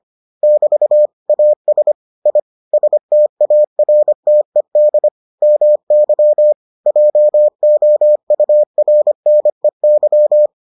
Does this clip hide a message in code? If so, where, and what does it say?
Yes: Morse code from 0.4 to 10.5 s, heard as a keyed tone, about 25 wpm, 600 Hz. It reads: <BT> AS I STARTED MY JOURNEY